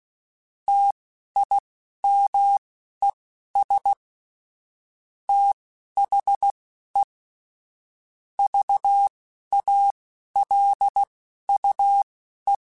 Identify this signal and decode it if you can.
Morse code, heard as a keyed tone, about 16 wpm, spaced as for 11 wpm, 800 Hz. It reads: TIMES THE VALUE